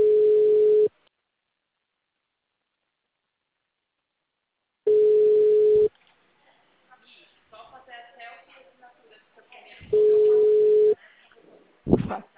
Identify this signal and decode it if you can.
Ringback tone (Europe, 425 Hz), with speech.
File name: out- -2007- - - .75122